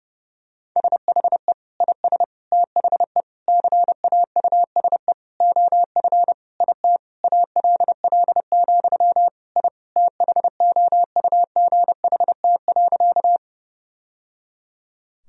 Morse code, heard as a keyed tone, about 30 wpm, 700 Hz. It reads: SHE IS THE CAUSE OF IT ALL, I THOUGHT.
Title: Sentences from Top 1000 Words - Review Part 1 - 30wpm